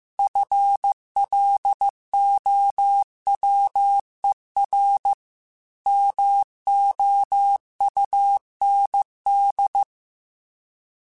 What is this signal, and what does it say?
Morse code, heard as a keyed tone, about 15 wpm, 800 Hz. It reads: FLOWER MOUND